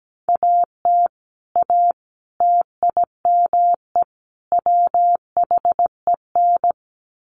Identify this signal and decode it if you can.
Morse code, heard as a keyed tone, about 17 wpm, 700 Hz. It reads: AT A TIME WHEN
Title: Sets of 4 Words 17wpm